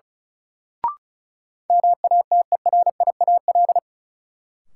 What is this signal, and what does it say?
Morse code, heard as a keyed tone, about 34 wpm, 700 Hz. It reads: MATERIAL
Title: Sentences from Top 1100 Words - New Word Drill Alt - 35wpm